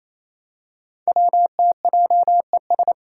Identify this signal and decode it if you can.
Morse code, heard as a keyed tone, about 27 wpm, 700 Hz. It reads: WTJES
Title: Code Groups - 5 Letters - 28wpm